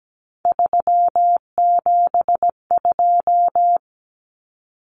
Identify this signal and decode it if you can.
Morse code, heard as a keyed tone, about 17 wpm, 700 Hz. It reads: 372
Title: Common Cut Numbers - 3-Digit 17wpm